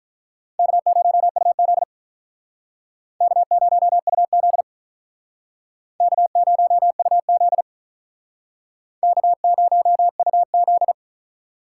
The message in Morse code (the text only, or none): K0UZ K0UZ K0UZ K0UZ